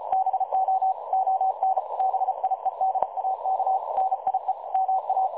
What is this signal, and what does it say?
Morse code, heard as a keyed tone, about 36 wpm, 760 Hz. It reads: BO ON BIEDAK NIE MA
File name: dlazaawansowanych / nagraniazpasma / websdr recording 3565.0kHz